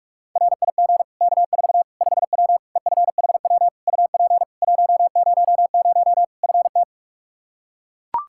Morse code, heard as a keyed tone, about 41 wpm, 700 Hz. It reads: RIG K4 5W EFHW UP 100 FT
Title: Practice QSO - Exchange 2 - Info - 45wpm